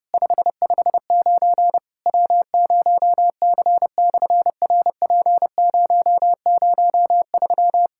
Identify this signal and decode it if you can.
Morse code, heard as a keyed tone, about 30 wpm, 700 Hz. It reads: 559 W0C/RP003
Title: SOTA QSO Practice 30wpm